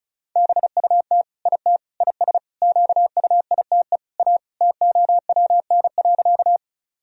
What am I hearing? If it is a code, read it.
Morse code, heard as a keyed tone, about 36 wpm, 700 Hz. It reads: BUT IT IS QUITE A TOWN.
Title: Sentences from Top 1000 Words - Review Part 2 - 35wpm